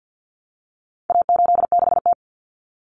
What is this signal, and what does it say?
Morse code, heard as a keyed tone, about 50 wpm, 700 Hz. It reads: A86T
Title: Common Cut Numbers - 4-Digit 50wpm